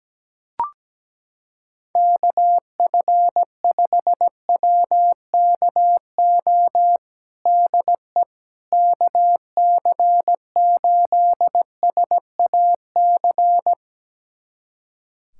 Morse code, heard as a keyed tone, about 17 wpm, 700 Hz. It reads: KF5WKO DE KC8SAC
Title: USA Callsign DE Callsign 17wpm